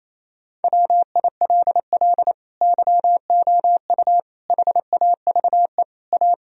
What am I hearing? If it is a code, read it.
Morse code, heard as a keyed tone, about 28 wpm, 700 Hz. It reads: WILL YOU HAVE A
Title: Sentences from Top 1000 Words - Review Part 1 - 28wpm